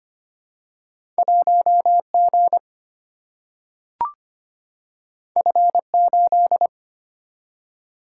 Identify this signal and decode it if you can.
Morse code, heard as a keyed tone, about 25 wpm, 700 Hz. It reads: 1G F8
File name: Two Letter-Number Combinations - 25wpm